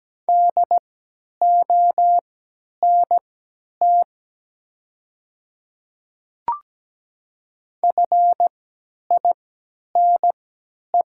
Morse code, heard as a keyed tone, about 17 wpm, spaced as for 10 wpm, 700 Hz. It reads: DONT FINE
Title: 3x Character Spacing - Top 500 Words - 17wpm